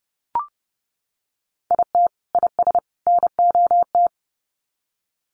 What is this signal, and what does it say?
Morse code, heard as a keyed tone, about 30 wpm, 700 Hz. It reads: IT IS NOT
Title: Sets of 3 Words 30wpm